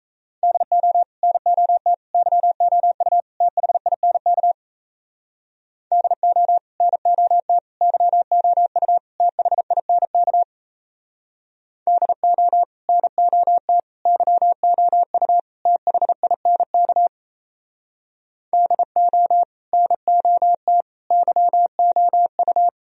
Morse code, heard as a keyed tone, about 36 wpm, 700 Hz. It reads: DO NOT YOU THINK DO NOT YOU THINK DO NOT YOU THINK DO NOT YOU